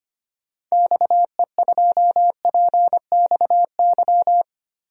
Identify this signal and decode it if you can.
Morse code, heard as a keyed tone, about 25 wpm, 700 Hz. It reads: XE2PXY
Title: DXCC-100 Call Signs - 6 Characters - Repeated 3x - 25wpm